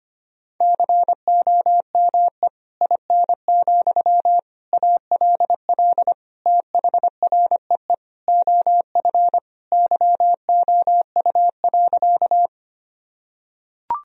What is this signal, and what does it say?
Morse code, heard as a keyed tone, about 25 wpm, 700 Hz. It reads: COME IN, ALL THREE OF YOU.
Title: Sentences from Top 1000 Words - Review Part 1 - 25wpm